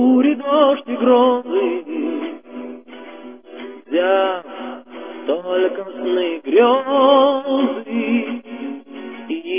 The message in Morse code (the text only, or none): none